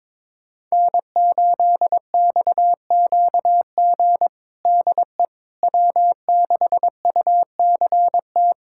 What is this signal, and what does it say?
Morse code, heard as a keyed tone, about 22 wpm, 700 Hz. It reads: N8XQG DE W6UCT
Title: USA Callsign DE Callsign - 5 Characters - 22wpm